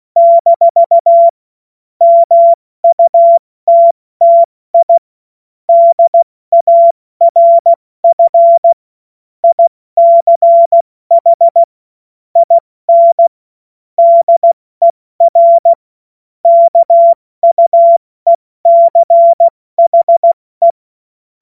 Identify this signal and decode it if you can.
Morse code, heard as a keyed tone, about 16 wpm, spaced as for 14 wpm, 680 Hz. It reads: - MUTTI DARF ICH IN DER KUECHE
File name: Morsekurs / Witze / Witz 07. September 2021, 70 F80